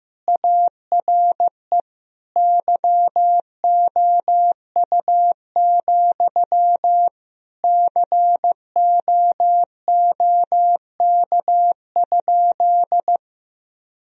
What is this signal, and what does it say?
Morse code, heard as a keyed tone, about 15 wpm, 700 Hz. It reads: ARE YOU, COOK?
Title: Sentences from Top 1000 Words - Review Part 4 - 15wpm